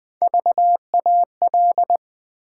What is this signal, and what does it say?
Morse code, heard as a keyed tone, about 20 wpm, 700 Hz. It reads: VAL
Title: Build A Word - 1 Letter At A Time - 5-Letter Words - 20wpm